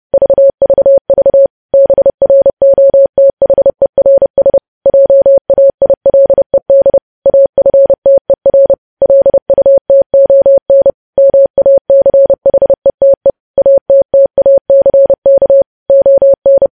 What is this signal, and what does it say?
Morse code, heard as a keyed tone, about 30 wpm, 550 Hz. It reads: VVV BROTHERS JAILED AFTER LUTON MACHETE ATTACK ON
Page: news headlines in morse code at 30 wpm